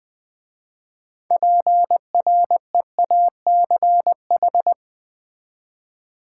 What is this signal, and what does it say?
Morse code, heard as a keyed tone, about 20 wpm, 700 Hz. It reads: PREACH
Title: 6 Letter Words - First 3 Letters Given - 20wpm